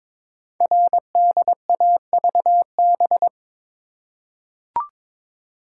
Morse code, heard as a keyed tone, about 22 wpm, 700 Hz. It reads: RDAVB
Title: Code Groups - 5 Letters - 22wpm